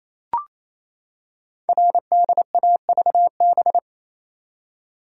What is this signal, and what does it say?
Morse code, heard as a keyed tone, about 27 wpm, 700 Hz. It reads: RDAVB
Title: Code Groups - 5 Letters - 28wpm